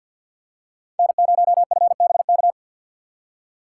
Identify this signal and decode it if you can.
Morse code, heard as a keyed tone, about 52 wpm, 700 Hz. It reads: N0FBK
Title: USA Call Signs - 3x Character Spacing - 50wpm